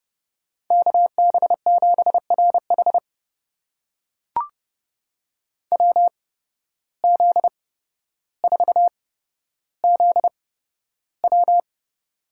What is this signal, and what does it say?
Morse code, heard as a keyed tone, about 31 wpm, spaced as for 13 wpm, 700 Hz. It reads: KB7RH WZ4ZW
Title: USA Call Signs - 5 Characters - 8x Character Spacing - 30wpm